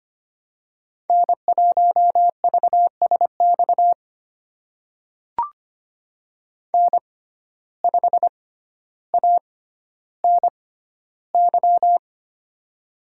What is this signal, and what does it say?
Morse code, heard as a keyed tone, about 25 wpm, spaced as for 13 wpm, 700 Hz. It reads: N1VSX N5ANY
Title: USA Call Signs - 5 Characters - 6x Character Spacing - 25wpm